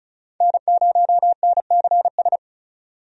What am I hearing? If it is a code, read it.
Morse code, heard as a keyed tone, about 36 wpm, 700 Hz. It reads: N0NCS
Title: USA Call Signs - 5 Characters - 35wpm